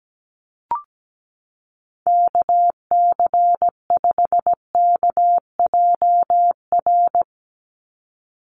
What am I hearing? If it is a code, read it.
Morse code, heard as a keyed tone, about 17 wpm, 700 Hz. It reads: KC5KJR
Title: USA Call Signs 17wpm